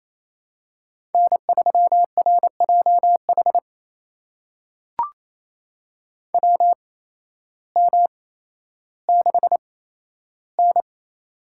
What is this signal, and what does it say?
Morse code, heard as a keyed tone, about 28 wpm, spaced as for 13 wpm, 700 Hz. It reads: N3RJH WM6N
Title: USA Call Signs - 8x Character Spacing - 28wpm